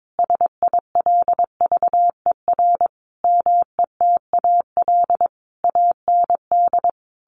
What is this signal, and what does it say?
Morse code, heard as a keyed tone, about 22 wpm, 700 Hz. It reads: SILVER METAL AND
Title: A Sound Of Thunder 22wpm